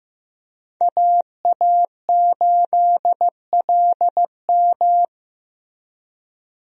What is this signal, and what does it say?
Morse code, heard as a keyed tone, about 15 wpm, 700 Hz. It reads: AA8LM
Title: USA Call Signs - 5 Characters - Repeated 2x - 15wpm